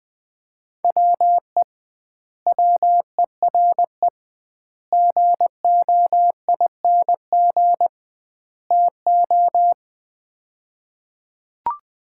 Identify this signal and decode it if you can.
Morse code, heard as a keyed tone, about 20 wpm, 700 Hz. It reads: WE WERE GOING TO
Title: 2x Word-Spacing - Sets of 4 Words - 20wpm